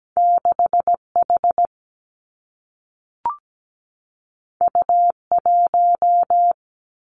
Morse code, heard as a keyed tone, about 17 wpm, 700 Hz. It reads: 6 H U 1
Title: Two Challenging Letter-Number Combinations 17wpm